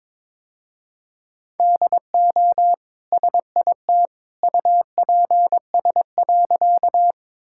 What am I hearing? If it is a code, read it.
Morse code, heard as a keyed tone, about 22 wpm, 700 Hz. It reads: DO SIT UPS.
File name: practice / 22wpm / Easy Sentences with Hints - 22wpm